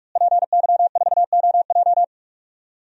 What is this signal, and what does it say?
Morse code, heard as a keyed tone, about 41 wpm, 700 Hz. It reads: PY4OJ
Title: DXCC-100 Call Signs - 5 Characters - Repeated 3x - 45wpm